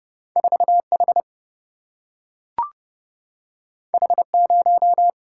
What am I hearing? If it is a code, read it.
Morse code, heard as a keyed tone, about 30 wpm, 700 Hz. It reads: 4 H H 0
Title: Single-Letter Single-Number 30wpm